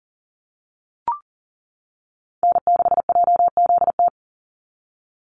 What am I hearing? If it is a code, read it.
Morse code, heard as a keyed tone, about 40 wpm, 700 Hz. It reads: N6JZT
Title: NA SOTA Activator Callsigns 40wpm